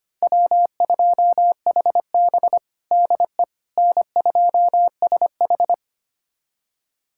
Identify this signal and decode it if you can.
Morse code, heard as a keyed tone, about 25 wpm, 700 Hz. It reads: W2HB DE N2SH
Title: USA Callsign DE Callsign - 4 Characters - 25wpm